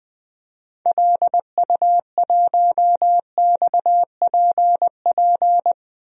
Morse code, heard as a keyed tone, about 20 wpm, 700 Hz. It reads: LU1XPP
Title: DXCC-100 Call Signs - 6 Characters - 20wpm